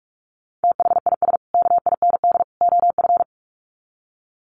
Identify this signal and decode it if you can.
Morse code, heard as a keyed tone, about 44 wpm, 700 Hz. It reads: THIS KIND OF
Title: Sets of 3 Words 45wpm